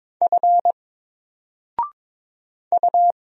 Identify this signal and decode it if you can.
Morse code, heard as a keyed tone, about 22 wpm, 700 Hz. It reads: F U
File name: Single-Letter-Number-22wpm